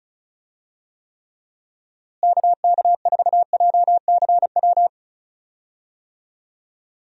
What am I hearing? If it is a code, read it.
Morse code, heard as a keyed tone, about 36 wpm, 700 Hz. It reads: KK4JCW
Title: USA Call Signs - 6 Characters - Repeated 2x - 35wpm